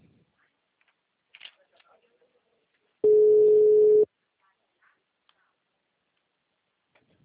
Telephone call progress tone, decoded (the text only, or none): ringback tone